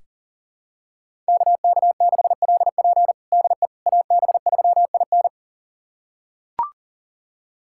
Morse code, heard as a keyed tone, about 40 wpm, 700 Hz. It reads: KK6LP DE AB3IN
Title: USA Callsign DE Callsign - 5 Characters - 40wpm